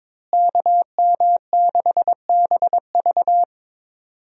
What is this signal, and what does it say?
Morse code, heard as a keyed tone, about 22 wpm, 700 Hz. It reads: KM6BV
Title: USA Call Signs - 5 Characters - Repeated 2x - 22wpm